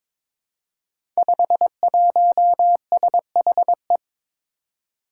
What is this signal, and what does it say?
Morse code, heard as a keyed tone, about 22 wpm, 700 Hz. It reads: 51SHE
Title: Five Challenging Letter-Number Combinations 22wpm